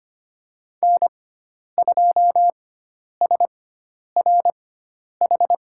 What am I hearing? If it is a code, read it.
Morse code, heard as a keyed tone, about 25 wpm, spaced as for 10 wpm, 700 Hz. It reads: N2SRH
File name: Lesson 16g - Letter c - Call Signs 25wpm